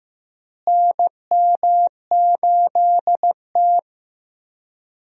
Morse code, heard as a keyed tone, about 15 wpm, 700 Hz. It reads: NM8T
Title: USA Call Signs - 4 Characters - Repeated 3x - 15wpm